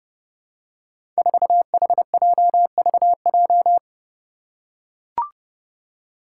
Morse code, heard as a keyed tone, about 32 wpm, 700 Hz. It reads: 4HJVJ
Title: Five Challenging Letter-Number Combinations 30wpm